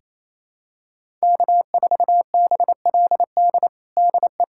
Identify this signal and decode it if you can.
Morse code, heard as a keyed tone, about 28 wpm, 700 Hz. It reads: K4BLD DE
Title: USA Callsign DE Callsign 28wpm